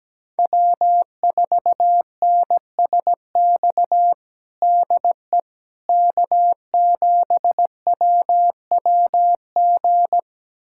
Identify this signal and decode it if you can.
Morse code, heard as a keyed tone, about 17 wpm, 700 Hz. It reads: W4NSX DE K7WWG
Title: USA Callsign DE Callsign 17wpm